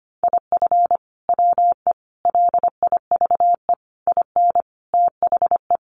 Morse code, heard as a keyed tone, about 25 wpm, 700 Hz. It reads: IF WE LIVE IN THE
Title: Sentences from Top 1400 Words 25wpm